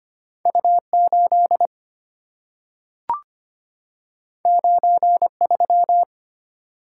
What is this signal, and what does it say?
Morse code, heard as a keyed tone, about 25 wpm, 700 Hz. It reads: U8 93
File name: Two Letter-Number Combinations - 25wpm